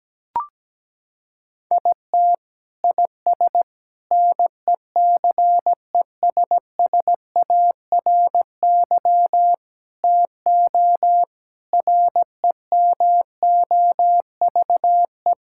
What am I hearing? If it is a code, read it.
Morse code, heard as a keyed tone, about 17 wpm, 700 Hz. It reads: IT IS NECESSARY TO REMOVE
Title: Sentences from Top 1000 Words - Review Part 5 - 17wpm